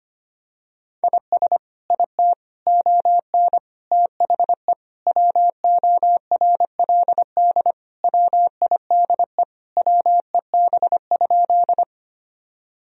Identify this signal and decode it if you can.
Morse code, heard as a keyed tone, about 25 wpm, 700 Hz. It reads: IS IT ON THE WORLD WIDE WEB?
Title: Sentences from Top 1000 Words - Review Part 4 - 25wpm